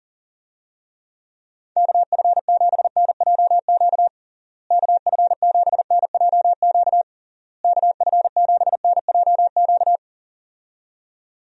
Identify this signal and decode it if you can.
Morse code, heard as a keyed tone, about 40 wpm, 700 Hz. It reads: KF7NJQ KF7NJQ KF7NJQ
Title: USA Call Signs - 6 Characters - Repeated 3x - 40wpm